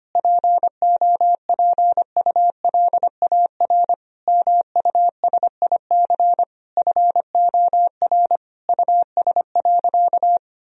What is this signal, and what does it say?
Morse code, heard as a keyed tone, about 25 wpm, 700 Hz. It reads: POPULAR MUSIC FOR US.
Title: Sentences from Top 1000 Words - Review Part 3 - 25wpm